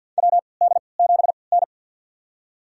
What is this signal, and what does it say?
Morse code, heard as a keyed tone, about 50 wpm, spaced as for 28 wpm, 700 Hz. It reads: WD7N
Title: USA Call Signs - 4 Characters - 3x Character Spacing - 50wpm